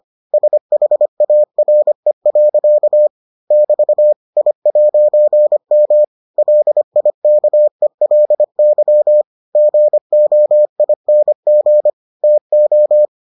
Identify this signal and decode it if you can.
Morse code, heard as a keyed tone, about 25 wpm, 600 Hz. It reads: SHARE. <BT> I'M LIKELY GOING TO